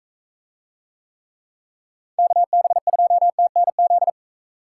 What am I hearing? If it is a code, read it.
Morse code, heard as a keyed tone, about 40 wpm, 700 Hz. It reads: KB2TNZ